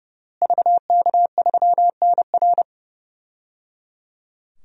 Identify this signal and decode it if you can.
Morse code, heard as a keyed tone, about 30 wpm, 700 Hz. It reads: VK3NR
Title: DXCC-100 Call Signs - 5 Characters - Repeated 3x - 30wpm